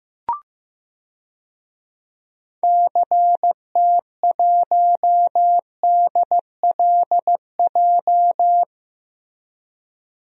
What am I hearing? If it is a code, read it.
Morse code, heard as a keyed tone, about 15 wpm, 700 Hz. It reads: CT1DLJ
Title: DXCC-100 Call Signs - 6 Characters - Repeated 3x - 15wpm